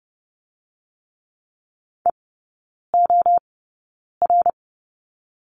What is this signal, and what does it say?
Morse code, heard as a keyed tone, about 30 wpm, 700 Hz. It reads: E O R
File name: Lesson 19e - Letter w - Three Characters 30wpm